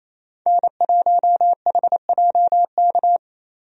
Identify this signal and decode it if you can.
Morse code, heard as a keyed tone, about 28 wpm, 700 Hz. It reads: N1HJK
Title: USA Call Signs - 5 Characters - Repeated 2x - 28wpm